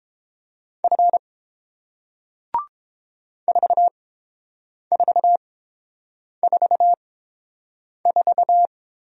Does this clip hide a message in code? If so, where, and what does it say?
Yes: Morse code from 0.8 to 8.7 s, heard as a keyed tone, about 30 wpm, spaced as for 7 wpm, 700 Hz. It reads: F 4444